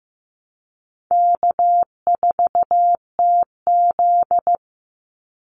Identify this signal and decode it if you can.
Morse code, heard as a keyed tone, about 15 wpm, 700 Hz. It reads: K4TZ
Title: Easy Call Signs Repeated 3x - 15wpm